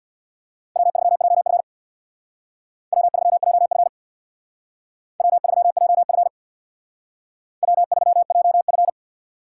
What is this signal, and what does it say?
Morse code, heard as a keyed tone, about 61 wpm, 700 Hz. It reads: W3JF W3JF W3JF W3JF
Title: Speed Racing - USA Call Signs 50wpm